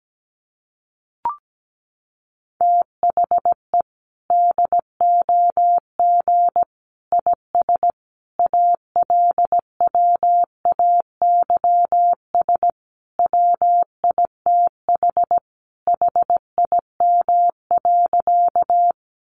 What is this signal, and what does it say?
Morse code, heard as a keyed tone, about 17 wpm, 700 Hz. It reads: THE DOG IS ALWAYS WITH HIM.
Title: Sentences from Top 400 Words 17wpm